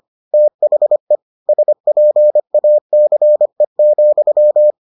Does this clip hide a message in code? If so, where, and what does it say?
Yes: Morse code from 0.3 to 4.7 s, heard as a keyed tone, about 25 wpm, 600 Hz. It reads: THE SPACE,